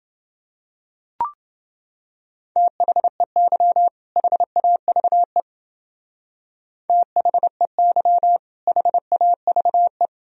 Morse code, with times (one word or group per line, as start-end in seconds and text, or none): 2.6-3.9 THEY
4.2-5.4 HAVE
6.9-8.4 THEY
8.7-10.1 HAVE